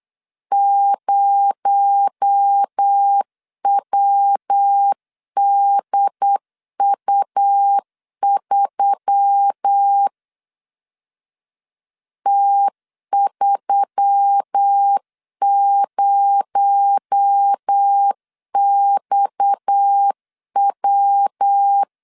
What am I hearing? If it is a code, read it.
Morse code, heard as a keyed tone, about 8 wpm, 800 Hz. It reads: 0WDU3 T30XW
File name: LEZIONI / 6 PKX0 / 6 PKX0 8wpm 3341